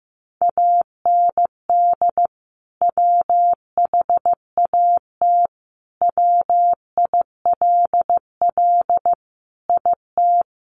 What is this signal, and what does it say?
Morse code, heard as a keyed tone, about 15 wpm, 700 Hz. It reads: AND WHAT WILL IT
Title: Sentences from Top 1200 Words 15wpm